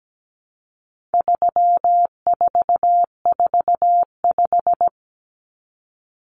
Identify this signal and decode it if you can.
Morse code, heard as a keyed tone, about 17 wpm, 700 Hz. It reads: 3445
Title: Numbers - 4-Digit 17wpm